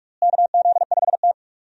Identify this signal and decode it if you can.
Morse code, heard as a keyed tone, about 45 wpm, 700 Hz. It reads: KZ5T
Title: USA Call Signs - 4 Characters - Repeated 2x - 45wpm